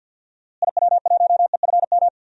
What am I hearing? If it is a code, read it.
Morse code, heard as a keyed tone, about 51 wpm, 700 Hz. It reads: IW1EFM
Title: DXCC-100 Call Signs - 6 Characters - Repeated 2x - 50wpm